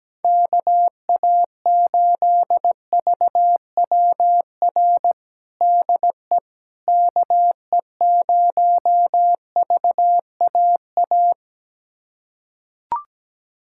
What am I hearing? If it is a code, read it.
Morse code, heard as a keyed tone, about 17 wpm, 700 Hz. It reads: KA8VWR DE KE0VAA
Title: USA Callsign DE Callsign - 6 Characters - 17wpm